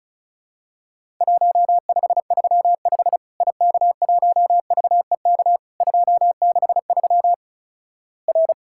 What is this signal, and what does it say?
Morse code, heard as a keyed tone, about 36 wpm, 700 Hz. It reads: 1535 IK1VEK 263 R
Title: Worked All Europe DX Contest - QTC Exchange - 35wpm